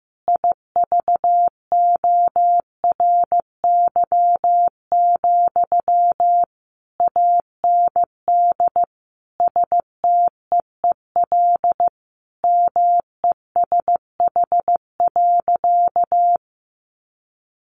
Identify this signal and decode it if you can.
Morse code, heard as a keyed tone, about 15 wpm, 700 Hz. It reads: IVORY, AND STEEL MESH.
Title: A Sound Of Thunder 15wpm